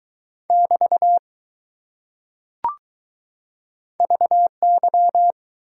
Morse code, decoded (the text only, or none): <BT> VY